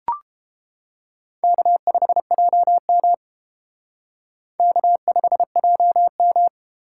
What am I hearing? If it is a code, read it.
Morse code, heard as a keyed tone, about 28 wpm, 700 Hz. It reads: K5JM K5JM